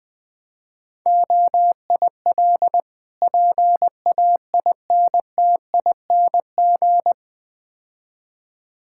Morse code, heard as a keyed tone, about 20 wpm, 700 Hz. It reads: OIL PAINTING
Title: Difficult Words - Speed-Racing with Hints - 20wpm